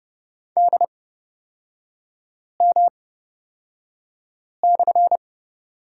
Morse code, heard as a keyed tone, about 30 wpm, spaced as for 5 wpm, 700 Hz. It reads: DM/
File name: Lesson 37e - Letter x - Three Characters 30wpm